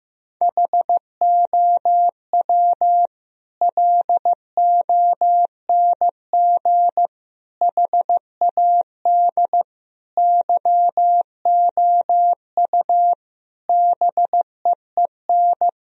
Morse code, heard as a keyed tone, about 15 wpm, 700 Hz. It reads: HOW LONG HAD YOU BEEN
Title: Sentences from Top 1000 Words - Review Part 3 - 15wpm